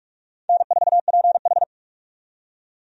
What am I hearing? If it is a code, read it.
Morse code, heard as a keyed tone, about 46 wpm, 700 Hz. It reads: N4PH
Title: USA Call Signs - 4 Characters - 45wpm